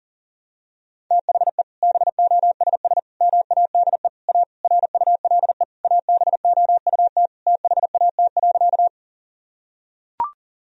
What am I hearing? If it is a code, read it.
Morse code, heard as a keyed tone, about 40 wpm, 700 Hz. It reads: THE BOSS MADE A RULE ABOUT THAT.
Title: Sentences from Top 1000 Words - Review Part 4 - 40wpm